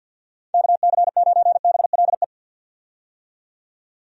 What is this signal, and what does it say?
Morse code, heard as a keyed tone, about 53 wpm, 700 Hz. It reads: KC9BLE
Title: USA Call Signs - 6 Characters - 50wpm